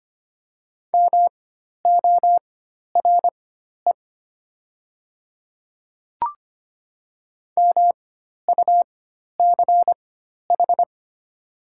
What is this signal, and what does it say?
Morse code, heard as a keyed tone, about 25 wpm, spaced as for 12 wpm, 700 Hz. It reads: MORE MUCH
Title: 4x Character Spacing - Top 300 Words - 25wpm